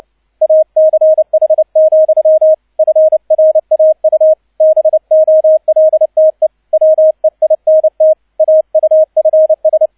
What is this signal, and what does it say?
Morse code, heard as a keyed tone, about 29 wpm, 630 Hz, with radio noise. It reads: ACH, FRAU BOLTE WEINT AUFS
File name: 002 Zweiter Streich 30wpm